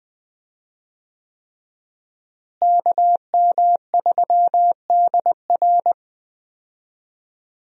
Morse code, heard as a keyed tone, about 20 wpm, 700 Hz. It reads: KM3DR
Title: USA Call Signs - 5 Characters - Repeated 3x - 20wpm